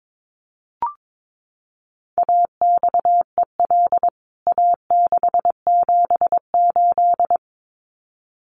Morse code, heard as a keyed tone, about 22 wpm, 700 Hz. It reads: AXEL A678
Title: CWT Exchange 22wpm